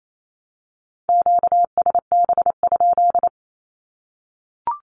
Morse code, heard as a keyed tone, about 28 wpm, 700 Hz. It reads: QSB?
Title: Q-Codes 28wpm